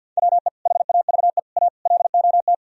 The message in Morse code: WE HAVE A LOT

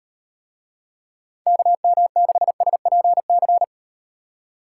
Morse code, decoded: KM6SPC

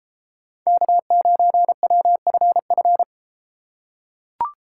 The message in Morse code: K9WFF

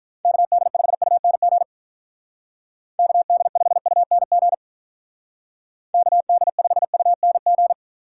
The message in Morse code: KD5UNG KD5UNG KD5UNG

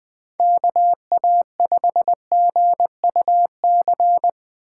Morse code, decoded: KA5GUC

